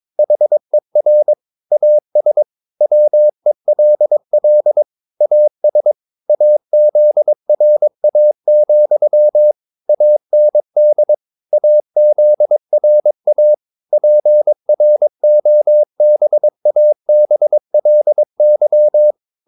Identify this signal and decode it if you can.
Morse code, heard as a keyed tone, about 22 wpm, 600 Hz. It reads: HER AS WELL AS AZRA, AND AZRA PROBABLY
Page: Romanen Never af Ken Follet på engelsk.